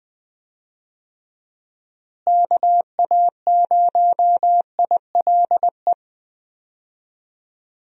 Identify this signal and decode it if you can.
Morse code, heard as a keyed tone, about 20 wpm, 700 Hz. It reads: KA0ILE
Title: USA Call Signs - 6 Characters - Repeated 2x - 20wpm